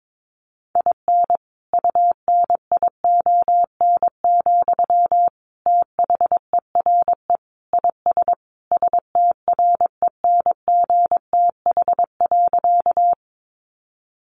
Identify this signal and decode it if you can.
Morse code, heard as a keyed tone, about 22 wpm, 700 Hz. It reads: IN UNION, THERE IS STRENGTH.
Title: Sentences from Top 1400 Words 22wpm